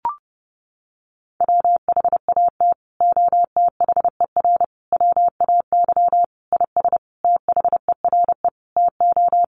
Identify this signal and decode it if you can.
Morse code, heard as a keyed tone, about 30 wpm, 700 Hz. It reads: WHAT OTHER WAY IS THERE TO
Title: Sentences from Top 100 Words 30wpm